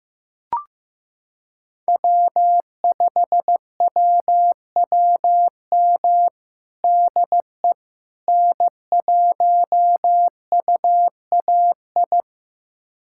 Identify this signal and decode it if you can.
Morse code, heard as a keyed tone, about 15 wpm, 700 Hz. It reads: W5WWM DE N1UAI
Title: USA Callsign DE Callsign - 5 Characters - 15wpm